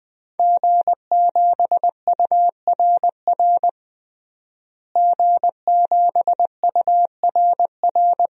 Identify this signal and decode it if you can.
Morse code, heard as a keyed tone, about 20 wpm, 700 Hz. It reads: G7URR G7URR
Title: DXCC-100 Call Signs - 5 Characters - Repeated 3x - 20wpm